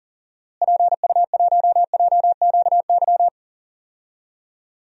Morse code, heard as a keyed tone, about 39 wpm, 700 Hz. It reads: PU1JQY
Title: DXCC-100 Call Signs - 6 Characters - Repeated 3x - 40wpm